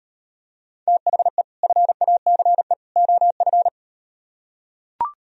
Morse code, heard as a keyed tone, about 38 wpm, 700 Hz. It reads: THE FACE OF